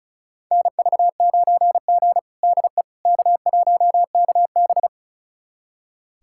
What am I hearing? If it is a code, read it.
Morse code, heard as a keyed tone, about 36 wpm, 700 Hz. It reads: NV9G DE K1KB
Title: USA Callsign DE Callsign - 4 Characters - 35wpm